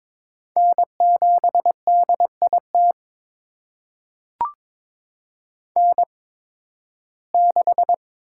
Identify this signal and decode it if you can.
Morse code, heard as a keyed tone, about 22 wpm, spaced as for 14 wpm, 700 Hz. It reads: N7DIT N6
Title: USA Call Signs - 5 Characters - 8x Character Spacing - 22wpm